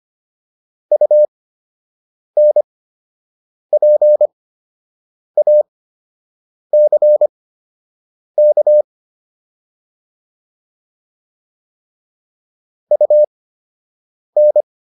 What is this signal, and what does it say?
Morse code, heard as a keyed tone, about 25 wpm, spaced as for 7 wpm, 600 Hz. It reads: UNPACK UN